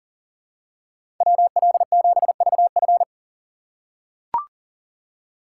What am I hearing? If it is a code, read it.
Morse code, heard as a keyed tone, about 38 wpm, 700 Hz. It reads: WL7VF
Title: USA Call Signs - 5 Characters - 6x Character Spacing - 40wpm